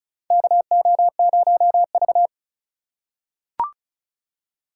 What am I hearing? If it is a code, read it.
Morse code, heard as a keyed tone, about 36 wpm, 700 Hz. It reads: KO0V
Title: USA Call Signs - 4 Characters - 4x Character Spacing - 35wpm